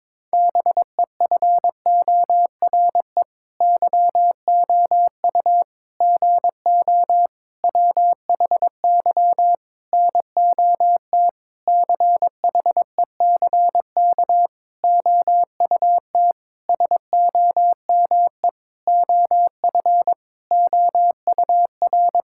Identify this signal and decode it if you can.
Morse code, heard as a keyed tone, about 22 wpm, 700 Hz. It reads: BEFORE YOU GO WHY NOT CHECK OUT SOME OF OUR
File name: Sentences with 9-Word Phrases - 22wpm